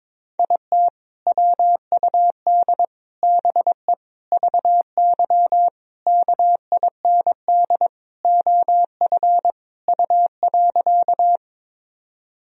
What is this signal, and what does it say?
Morse code, heard as a keyed tone, about 21 wpm, 700 Hz. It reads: IT WUD BE VY KIND OF U.
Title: Sentences from Top 200 Words and Abbreviations 22wpm